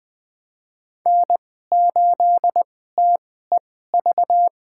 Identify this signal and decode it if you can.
Morse code, heard as a keyed tone, about 20 wpm, spaced as for 14 wpm, 700 Hz. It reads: N8TEV
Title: USA Call Signs - 5 Characters - 2x Character Spacing - 20wpm